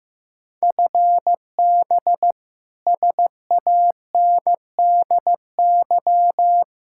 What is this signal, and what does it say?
Morse code, heard as a keyed tone, about 15 wpm, 700 Hz. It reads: FB SANDY
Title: Practice QSO - Exchange 3 - More Info - 15wpm